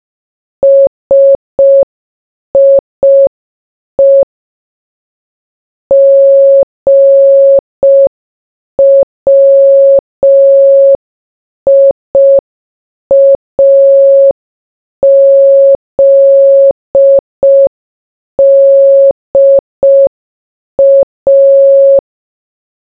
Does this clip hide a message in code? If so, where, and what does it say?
Yes: Morse code from 0.6 to 22.0 s, heard as a keyed tone, about 5 wpm, 550 Hz. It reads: SIE GWIAZDA